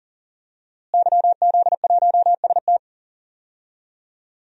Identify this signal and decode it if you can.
Morse code, heard as a keyed tone, about 40 wpm, 700 Hz. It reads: YZ1ST
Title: DXCC-100 Call Signs - 5 Characters - Repeated 3x - 40wpm